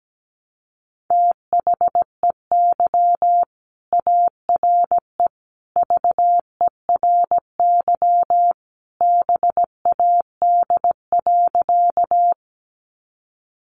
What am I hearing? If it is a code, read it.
Morse code, heard as a keyed tone, about 17 wpm, 700 Hz. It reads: THEY ARE VERY BAD.
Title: Sentences from Top 700 Words 17wpm